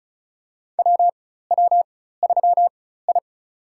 Morse code, heard as a keyed tone, about 35 wpm, spaced as for 16 wpm, 700 Hz. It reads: WW3I
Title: USA Call Signs - 4 Characters - 4x Character Spacing - 35wpm